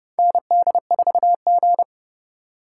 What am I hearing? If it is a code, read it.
Morse code, heard as a keyed tone, about 30 wpm, 700 Hz. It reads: ND4G
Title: K1USN SST QSO Practice - 30wpm